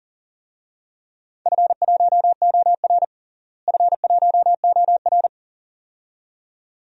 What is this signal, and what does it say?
Morse code, heard as a keyed tone, about 40 wpm, 700 Hz. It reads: F1OR F1OR
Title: DXCC-100 Call Signs - 4 Characters - Repeated 2x - 40wpm